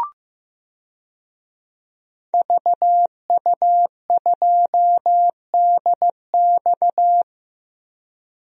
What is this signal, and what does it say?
Morse code, heard as a keyed tone, about 16 wpm, 700 Hz. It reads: VU2DX
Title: DXCC-100 Call Signs - 5 Characters - 15wpm